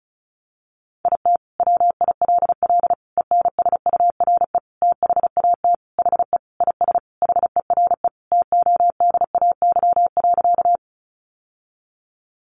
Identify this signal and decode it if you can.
Morse code, heard as a keyed tone, about 34 wpm, 700 Hz. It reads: IT WILL ENSURE THAT HE IS HERE TODAY.
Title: Sentences from Top 1100 Words 35wpm